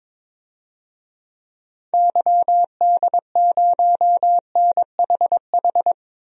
Morse code, heard as a keyed tone, about 22 wpm, 700 Hz. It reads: YD0NHH
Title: DXCC-100 Call Signs - 6 Characters - Repeated 2x - 22wpm